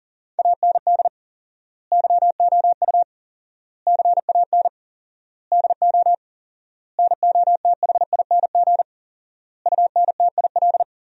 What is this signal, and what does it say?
Morse code, heard as a keyed tone, about 37 wpm, 700 Hz. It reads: AND YOU CAN DO NOTHING UNTIL
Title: 4x Word-Spacing - Sentences from Top 500 Words - 40wpm